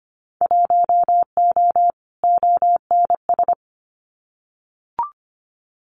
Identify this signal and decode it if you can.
Morse code, heard as a keyed tone, about 25 wpm, 700 Hz. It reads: 1O ONS
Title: Winter Field Day Exchanges Encore 25wpm